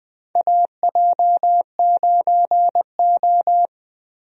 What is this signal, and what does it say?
Morse code, heard as a keyed tone, about 20 wpm, 700 Hz. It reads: AJ9O
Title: USA Call Signs - 4 Characters - 20wpm